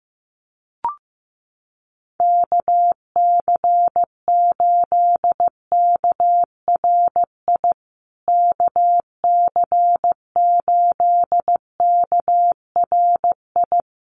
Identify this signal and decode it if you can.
Morse code, heard as a keyed tone, about 15 wpm, 700 Hz. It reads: KC8KRI KC8KRI
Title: USA Call Signs Repeated 3x 15wpm